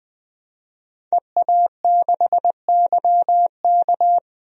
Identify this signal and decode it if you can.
Morse code, heard as a keyed tone, about 21 wpm, 700 Hz. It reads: EA6YK
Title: DXCC-100 Call Signs - 5 Characters - Repeated 3x - 20wpm